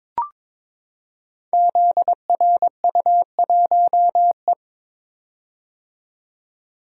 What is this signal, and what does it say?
Morse code, heard as a keyed tone, about 22 wpm, 700 Hz. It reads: ZRU1E
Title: Code Groups - 5 Letter:Numbers - 22wpm